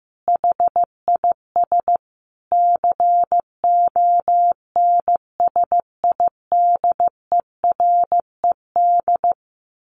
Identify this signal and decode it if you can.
Morse code, heard as a keyed tone, about 15 wpm, 700 Hz. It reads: HIS CONSIDERED
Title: A Sound Of Thunder 15wpm